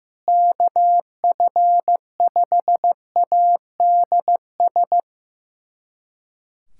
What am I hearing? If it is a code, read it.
Morse code, heard as a keyed tone, about 16 wpm, 700 Hz. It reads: KF5ADS
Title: USA Callsign DE Callsign 15wpm